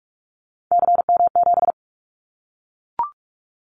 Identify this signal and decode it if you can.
Morse code, heard as a keyed tone, about 44 wpm, 700 Hz. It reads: CM7
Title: Callsign Prefixes for Top 100 DXCC - 45wpm